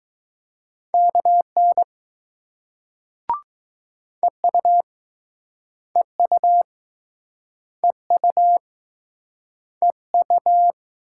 Morse code, heard as a keyed tone, about 20 wpm, 700 Hz. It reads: KN EU EU EU EU